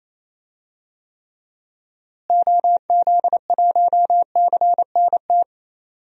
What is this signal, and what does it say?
Morse code, heard as a keyed tone, about 28 wpm, 700 Hz. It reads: OZ1CNT
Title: DXCC-100 Call Signs - 6 Characters - Repeated 2x - 28wpm